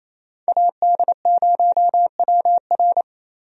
Morse code, heard as a keyed tone, about 28 wpm, 700 Hz. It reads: AD0WR
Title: USA Call Signs - 5 Characters - Repeated 3x - 28wpm